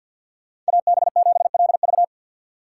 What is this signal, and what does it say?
Morse code, heard as a keyed tone, about 48 wpm, 700 Hz. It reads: AB7LV
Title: USA Call Signs - 4x Character Spacing - 50wpm